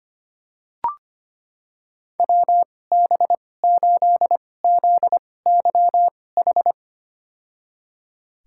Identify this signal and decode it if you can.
Morse code, heard as a keyed tone, about 25 wpm, spaced as for 18 wpm, 700 Hz. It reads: WB8ZYH
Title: USA Call Signs - 6 Characters - 2x Character Spacing - 25wpm